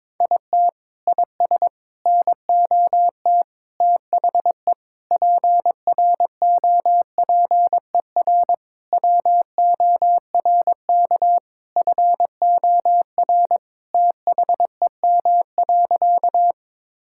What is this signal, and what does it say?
Morse code, heard as a keyed tone, about 22 wpm, 700 Hz. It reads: IT IS NOT THE PROPER WORK FOR THEM.
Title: Sentences from Top 1000 Words - Review Part 5 - 22wpm